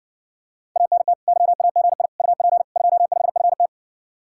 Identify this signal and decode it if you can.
Morse code, heard as a keyed tone, about 60 wpm, 700 Hz. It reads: ANT YAGI UP 25FT